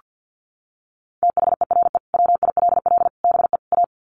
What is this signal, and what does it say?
Morse code, heard as a keyed tone, about 48 wpm, 700 Hz. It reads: THERE WILL BE A
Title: Sets of 4 Words 50wpm